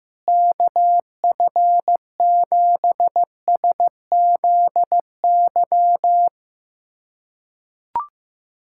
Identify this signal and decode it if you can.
Morse code, heard as a keyed tone, about 15 wpm, 700 Hz. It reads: KF7SZY
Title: USA Callsign DE Callsign - 6 Characters - 15wpm